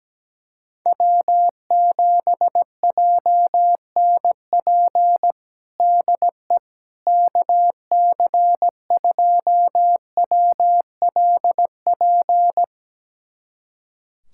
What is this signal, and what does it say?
Morse code, heard as a keyed tone, about 17 wpm, 700 Hz. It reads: W7JNP DE KC2WLP
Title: USA Callsign DE Callsign 17wpm